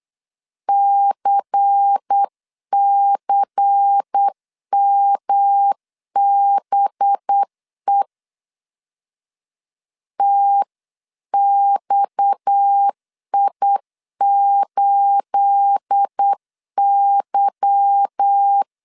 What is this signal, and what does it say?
Morse code, heard as a keyed tone, about 8 wpm, 800 Hz. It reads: CCMBE TXI8Y
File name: CW / DET / 1 9 1500C S C 8wpm 5653